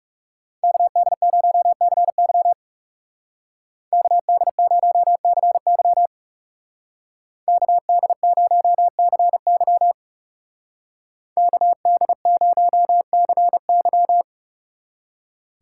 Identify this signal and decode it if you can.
Morse code, heard as a keyed tone, about 37 wpm, 700 Hz. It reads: KD0CY KD0CY KD0CY KD0CY
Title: Speed Racing - USA 2x2 Call Signs 30wpm